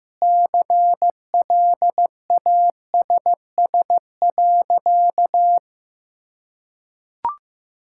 Morse code, heard as a keyed tone, about 15 wpm, 700 Hz. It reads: CLASS.
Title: Sentences from Top 1000 Words - Review Part 5 - 15wpm